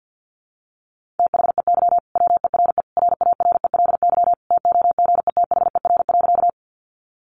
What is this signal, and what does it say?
Morse code, heard as a keyed tone, about 50 wpm, 700 Hz. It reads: THEY WERE RARELY TOGETHER.
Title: Sentences from Top 1400 Words 50wpm